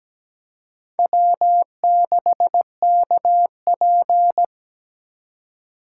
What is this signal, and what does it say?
Morse code, heard as a keyed tone, about 17 wpm, 700 Hz. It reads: W6KP
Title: USA Call Signs - 4 Characters - 4x Character Spacing - 17wpm